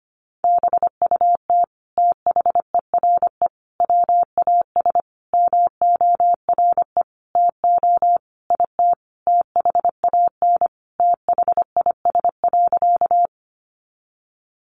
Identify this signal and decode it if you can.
Morse code, heard as a keyed tone, about 25 wpm, 700 Hz. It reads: BUT THERE WAS MORE TO IT THAN THIS.
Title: Sentences from Top 100 Words 25wpm